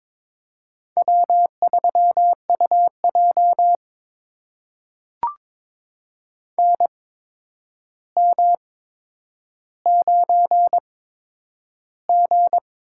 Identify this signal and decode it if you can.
Morse code, heard as a keyed tone, about 22 wpm, spaced as for 9 wpm, 700 Hz. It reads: W3UJ NM9G
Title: USA Call Signs - 4 Characters - 8x Character Spacing - 22wpm